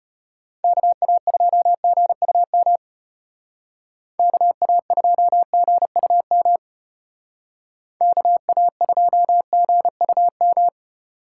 Morse code, heard as a keyed tone, about 33 wpm, 700 Hz. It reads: KA2GUM KA2GUM KA2GUM